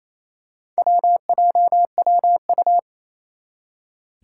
Morse code, heard as a keyed tone, about 28 wpm, 700 Hz. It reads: WJWU
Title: Four Challenging Letter-Number Combinations 28wpm